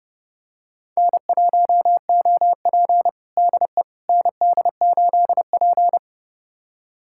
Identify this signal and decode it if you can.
Morse code, heard as a keyed tone, about 32 wpm, 700 Hz. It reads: N1OP DE ND8P
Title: USA Callsign DE Callsign - 4 Characters - 30wpm